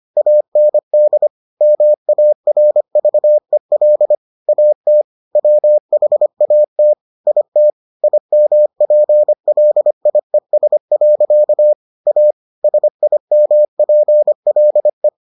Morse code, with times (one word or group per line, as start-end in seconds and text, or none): 0.2-1.3 AND
1.6-4.2 MARVEL
4.5-5.0 AT
5.4-6.9 WHAT
7.3-7.7 IT
8.0-11.7 IMPLIES.
12.1-12.3 A
12.6-15.1 SIMPLE